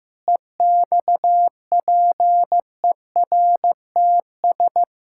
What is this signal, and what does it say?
Morse code, heard as a keyed tone, about 15 wpm, 700 Hz. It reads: EXPERTS
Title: Pangram Challenge v2 - 15wpm